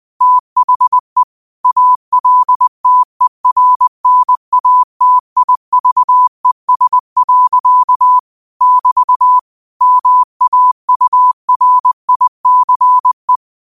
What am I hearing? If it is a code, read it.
Morse code, heard as a keyed tone, about 20 wpm, 1000 Hz. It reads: THE ALTERNATIVES. <BT> MAURICE